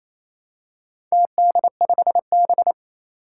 Morse code, heard as a keyed tone, about 28 wpm, 700 Hz. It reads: TD5B
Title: Four Challenging Letter-Number Combinations 28wpm